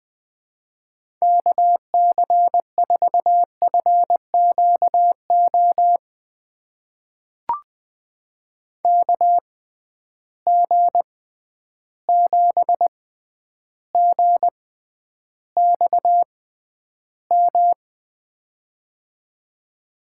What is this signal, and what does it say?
Morse code, heard as a keyed tone, about 20 wpm, spaced as for 10 wpm, 700 Hz. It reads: KC4FQO KG7GXM
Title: USA Call Signs - 6 Characters - 6x Character Spacing - 20wpm